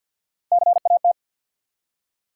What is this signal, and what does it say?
Morse code, heard as a keyed tone, about 48 wpm, 700 Hz. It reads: CAT
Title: Three Letter Words - 50wpm